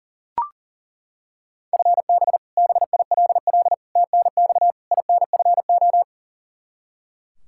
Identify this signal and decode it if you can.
Morse code, heard as a keyed tone, about 40 wpm, 700 Hz. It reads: FB BILL TNX INFO
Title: Practice QSO - Exchange 3 - More Info - 40wpm